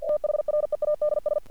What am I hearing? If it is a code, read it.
Morse code, heard as a keyed tone, about 48 wpm, 600 Hz. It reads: THREADS
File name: archived / CRSnet